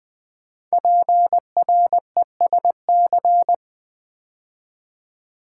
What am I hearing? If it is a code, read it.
Morse code, heard as a keyed tone, about 20 wpm, 700 Hz. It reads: PRESC
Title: Build A Word - 5 Letters At A Time - 12-Letter Words - 20wpm